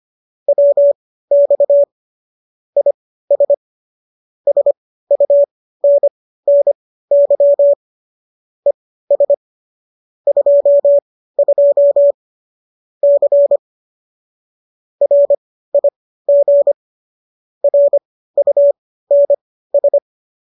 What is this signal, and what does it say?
Morse code, heard as a keyed tone, about 25 wpm, spaced as for 15 wpm, 570 Hz. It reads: WX IS SUNNY ES 22 C RIG RUNS